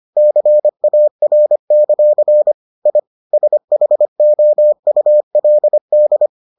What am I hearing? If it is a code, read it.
Morse code, heard as a keyed tone, about 25 wpm, 600 Hz. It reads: CAR; I SHOULD